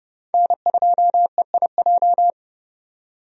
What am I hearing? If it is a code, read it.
Morse code, heard as a keyed tone, about 30 wpm, 700 Hz. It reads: N2EIJ
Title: USA Call Signs - 5 Characters - 30wpm